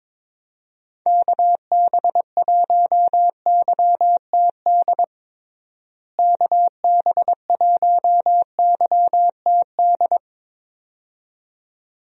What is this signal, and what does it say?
Morse code, heard as a keyed tone, about 22 wpm, 700 Hz. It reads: KB1YTD KB1YTD
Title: USA Call Signs - 6 Characters - Repeated 2x - 22wpm